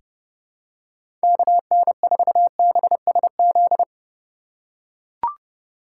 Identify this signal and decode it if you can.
Morse code, heard as a keyed tone, about 30 wpm, 700 Hz. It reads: KN4BSZ
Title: USA Call Signs - 6 Characters - 3x Character Spacing - 30wpm